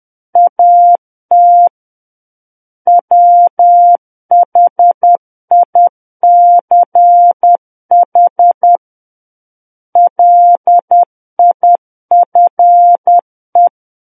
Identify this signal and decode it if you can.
Morse code, heard as a keyed tone, about 10 wpm, 700 Hz. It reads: AT WHICH LIFE
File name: Book One / 01-Chapter 1 - 10 WPM